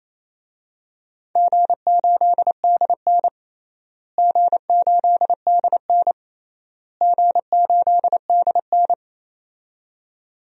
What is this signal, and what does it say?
Morse code, heard as a keyed tone, about 28 wpm, 700 Hz. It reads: G8DN G8DN G8DN
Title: DXCC-100 Call Signs - 4 Characters - Repeated 3x - 28wpm